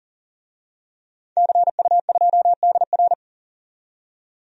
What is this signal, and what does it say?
Morse code, heard as a keyed tone, about 40 wpm, 700 Hz. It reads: CU2DR
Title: DXCC-100 Call Signs - 5 Characters - 40wpm